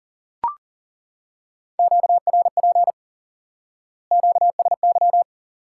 Morse code, heard as a keyed tone, about 40 wpm, 700 Hz. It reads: QRP QSY
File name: IQE Course 3 - Lesson 89 - YAGI - Two Elements - 40wpm